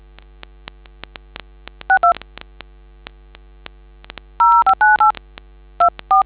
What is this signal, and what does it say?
DTMF, heard as telephone touch tones, tones of about 120 ms. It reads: 62 0598 24